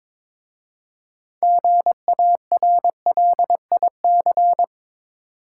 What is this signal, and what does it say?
Morse code, heard as a keyed tone, about 22 wpm, 700 Hz. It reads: GARLIC
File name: Difficult Words with Hints - 22wpm